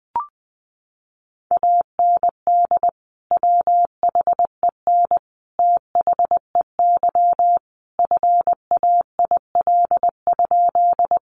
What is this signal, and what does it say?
Morse code, heard as a keyed tone, about 20 wpm, 700 Hz. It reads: AND WHEN THEY FAIL?
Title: Sentences from Top 1100 Words 20wpm